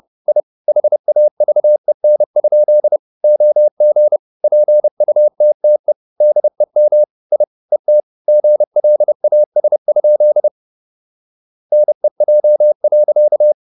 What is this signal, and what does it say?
Morse code, heard as a keyed tone, about 30 wpm, 600 Hz. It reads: I HAVEN? OG PUTTE DEM I ET GLAS? NEJ.
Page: Ferie 30 wpm